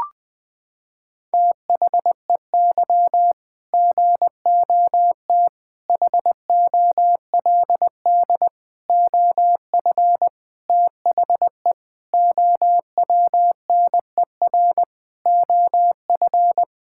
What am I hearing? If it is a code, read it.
Morse code, heard as a keyed tone, about 20 wpm, 700 Hz. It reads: THEY GOT HOLD OF THE OWNER OF
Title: Sentences from Top 1600 Words 20wpm